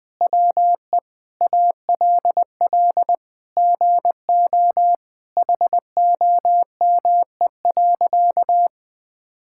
Morse code, heard as a keyed tone, about 20 wpm, 700 Hz. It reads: WE ALL GO HOME.
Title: Sentences with 4-Word Phrases - 20wpm